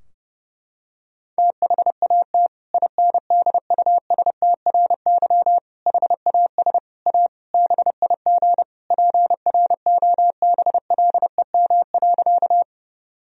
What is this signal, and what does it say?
Morse code, heard as a keyed tone, about 32 wpm, 700 Hz. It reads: THAT INDUSTRY HAS A BIG PROBLEM.
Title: Sentences from Top 1000 Words - Review Part 4 - 30wpm